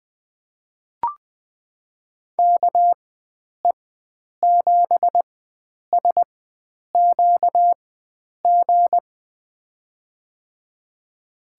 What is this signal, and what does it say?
Morse code, heard as a keyed tone, about 20 wpm, spaced as for 9 wpm, 700 Hz. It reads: KE7SQG
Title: USA Call Signs - 6 Characters - 4x Character Spacing - 20wpm